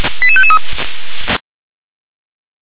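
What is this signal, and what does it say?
Five-tone selective call (ZVEI), heard as quick runs of five tones, tones of about 70 ms. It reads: 88553
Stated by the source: ZVEI five-tone call 88836 received by SDR.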